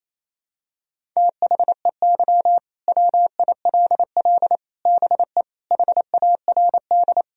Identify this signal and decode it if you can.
Morse code, heard as a keyed tone, about 28 wpm, 700 Hz. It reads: THEY WILL BE HARD
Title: Sentences from Top 1000 Words - Review Part 1 - 28wpm